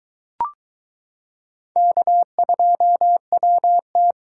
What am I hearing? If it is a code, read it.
Morse code, heard as a keyed tone, about 22 wpm, 700 Hz. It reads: K2WT